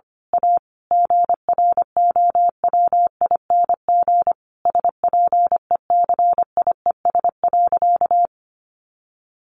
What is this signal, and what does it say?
Morse code, heard as a keyed tone, about 25 wpm, 700 Hz. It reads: A GROWING SPECIES.
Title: A Sound Of Thunder 25wpm